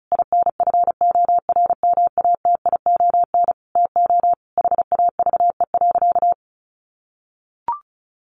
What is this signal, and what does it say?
Morse code, heard as a keyed tone, about 34 wpm, 700 Hz. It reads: INFORMATION TO HAVE.
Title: Sentences from Top 1400 Words 35wpm